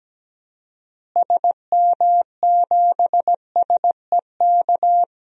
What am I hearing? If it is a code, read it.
Morse code, heard as a keyed tone, about 17 wpm, 700 Hz. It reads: SM7SEK
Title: DXCC-100 Call Signs - 6 Characters - Repeated 3x - 17wpm